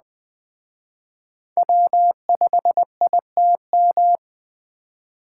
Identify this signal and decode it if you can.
Morse code, heard as a keyed tone, about 21 wpm, 700 Hz. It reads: W5ITM